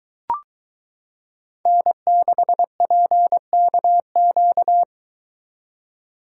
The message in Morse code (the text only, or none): N6PKQ